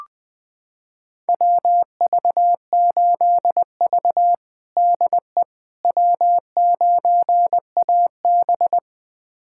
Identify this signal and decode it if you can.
Morse code, heard as a keyed tone, about 20 wpm, 700 Hz. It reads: WV8V DE W9AB
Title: USA Callsign DE Callsign - 4 Characters - 20wpm